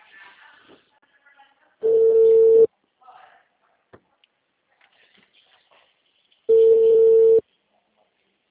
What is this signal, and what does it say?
Ringback tone (China, 450 Hz), with speech.
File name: out- -2005